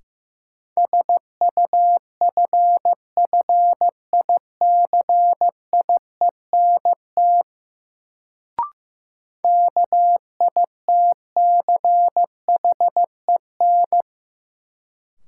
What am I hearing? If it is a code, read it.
Morse code, heard as a keyed tone, about 15 wpm, 700 Hz. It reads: SUFFICIENT KITCHEN
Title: Sentences from Top 1400 Words - New Word Drill - 15wpm